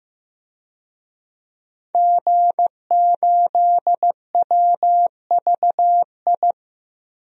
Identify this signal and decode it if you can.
Morse code, heard as a keyed tone, about 15 wpm, 700 Hz. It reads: G8WVI
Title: DXCC-100 Call Signs - 5 Characters - 15wpm